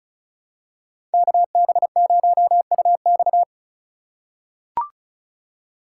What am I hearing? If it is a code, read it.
Morse code, heard as a keyed tone, about 36 wpm, 700 Hz. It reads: KB0UX
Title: USA Call Signs - 5 Characters - 2x Character Spacing - 35wpm